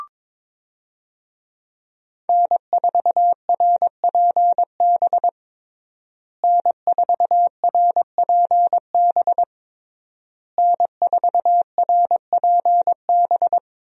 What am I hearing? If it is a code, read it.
Morse code, heard as a keyed tone, about 22 wpm, 700 Hz. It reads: N4RPB N4RPB N4RPB
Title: USA Call Signs - 5 Characters - Repeated 3x - 22wpm